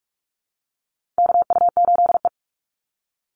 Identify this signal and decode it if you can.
Morse code, heard as a keyed tone, about 44 wpm, 700 Hz. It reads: KU8E
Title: Call Signs Repeated 3x - 45wpm